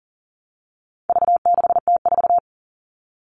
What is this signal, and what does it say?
Morse code, heard as a keyed tone, about 40 wpm, 700 Hz. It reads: V6T4
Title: All Cut Numbers - 4-Digit 40wpm